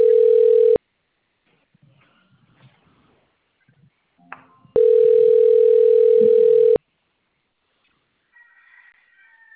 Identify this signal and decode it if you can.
Ringback tone (North America), 440 + 480 Hz.